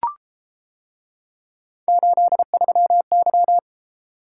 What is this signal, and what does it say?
Morse code, heard as a keyed tone, about 31 wpm, 700 Hz. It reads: 83Y